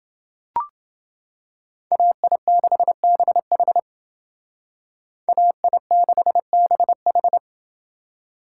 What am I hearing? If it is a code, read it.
Morse code, heard as a keyed tone, about 28 wpm, 700 Hz. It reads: AI6BH AI6BH